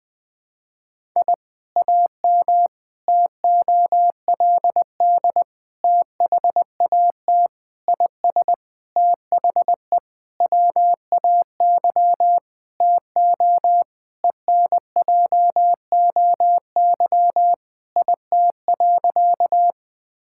Morse code, heard as a keyed tone, about 21 wpm, 700 Hz. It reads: I AM TOLD THAT IS THE WAY TO ENJOY IT.
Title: Sentences from Top 1000 Words - Review Part 5 - 20wpm